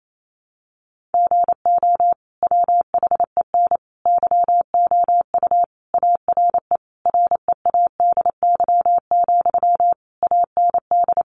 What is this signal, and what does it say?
Morse code, heard as a keyed tone, about 28 wpm, 700 Hz. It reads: GO WHEN YOU ARE READY, AND
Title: Sentences from Top 500 Words 28wpm